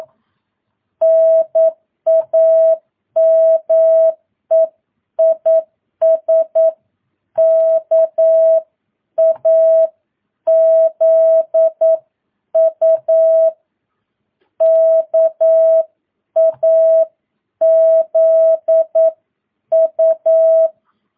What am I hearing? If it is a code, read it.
Morse code, heard as a keyed tone, about 9 wpm, 650 Hz. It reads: NAME IS KAZU KAZU
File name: jh8cbh / 2022cw